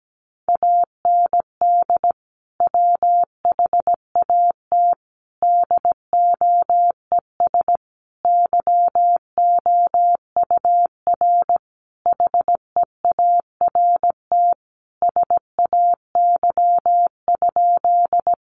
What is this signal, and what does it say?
Morse code, heard as a keyed tone, about 17 wpm, 700 Hz. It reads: AND WHAT DOES YOUR HEART SAY?
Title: Sentences from Top 1200 Words 17wpm